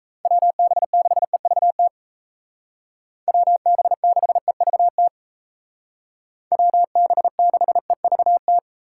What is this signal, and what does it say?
Morse code, heard as a keyed tone, about 35 wpm, 700 Hz. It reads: WB6EVT WB6EVT WB6EVT
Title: Speed Racing - USA 2x3 Call Signs 28wpm